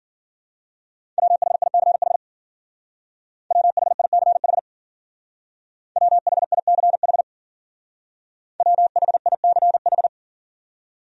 Morse code, heard as a keyed tone, about 47 wpm, 700 Hz. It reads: WHICH WHICH WHICH WHICH
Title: Speed-Racing Top 200 Words 40wpm